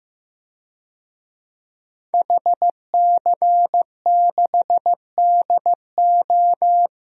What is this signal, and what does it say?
Morse code, heard as a keyed tone, about 15 wpm, 700 Hz. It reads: HC6DO
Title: DXCC-100 Call Signs - 5 Characters - Repeated 3x - 15wpm